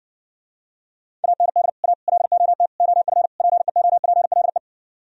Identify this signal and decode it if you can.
Morse code, heard as a keyed tone, about 60 wpm, 700 Hz. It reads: AND A LOT OF PEOPLE